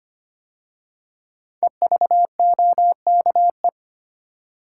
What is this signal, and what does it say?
Morse code, heard as a keyed tone, about 25 wpm, 700 Hz. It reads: EVOKE
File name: 5 Letter Words - First 3 Letters Given - 25wpm